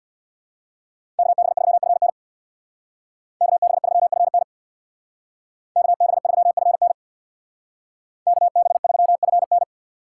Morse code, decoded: KB3FN KB3FN KB3FN KB3FN

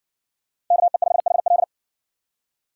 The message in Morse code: KE5HL

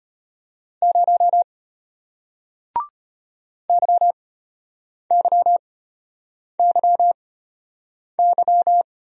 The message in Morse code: 0 YYYY

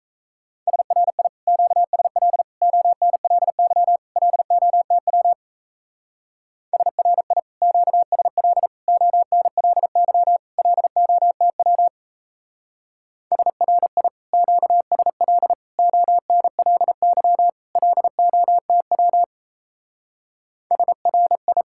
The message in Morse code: SRI QSL ONLY LOTW SRI QSL ONLY LOTW SRI QSL ONLY LOTW SRI

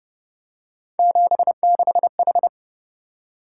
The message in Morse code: 76H

